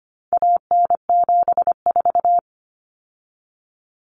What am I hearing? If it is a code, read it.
Morse code, heard as a keyed tone, about 25 wpm, 700 Hz. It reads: AN74
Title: Common Cut Numbers - 4-Digit 25wpm